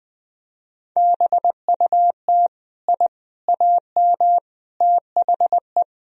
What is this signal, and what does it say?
Morse code, heard as a keyed tone, about 20 wpm, 700 Hz. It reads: BUT I AM THE
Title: Sentences from Top 1000 Words - Review Part 4 - 20wpm